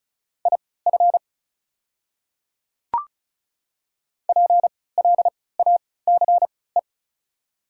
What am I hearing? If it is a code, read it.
Morse code, heard as a keyed tone, about 34 wpm, spaced as for 20 wpm, 690 Hz. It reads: IF PLACE
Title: 3x Character Spacing - Top 200 Words - 35wpm